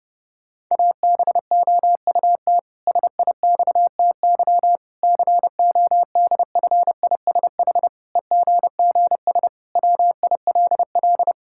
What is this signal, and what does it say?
Morse code, heard as a keyed tone, about 30 wpm, 700 Hz. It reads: ABOUT SIXTY CODFISH EGGS WILL
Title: Pangram Challenge v2 - 30wpm